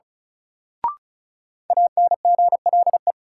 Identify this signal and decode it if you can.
Morse code, heard as a keyed tone, about 36 wpm, 700 Hz. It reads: ANGLE
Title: Sentences from Top 1500 Words - New Word Drill - 35wpm